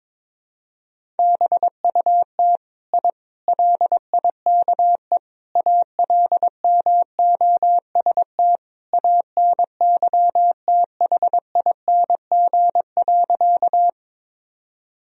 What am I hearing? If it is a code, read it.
Morse code, heard as a keyed tone, about 22 wpm, 700 Hz. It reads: BUT I LIKE ALMOST ANYTHING.
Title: Sentences from Top 1000 Words - Review Part 4 - 22wpm